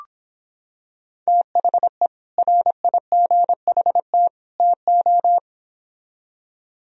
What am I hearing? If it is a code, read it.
Morse code, heard as a keyed tone, about 25 wpm, 700 Hz. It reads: THE RIGHT TO